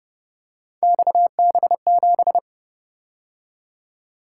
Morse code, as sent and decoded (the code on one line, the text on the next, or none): -..- -... --...
XB7